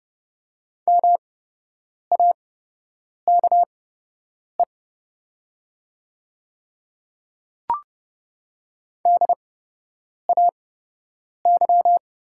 Morse code, heard as a keyed tone, about 30 wpm, spaced as for 8 wpm, 700 Hz. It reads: MAKE DAY
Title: 8x Character Spacing - Top 100 Words - 30wpm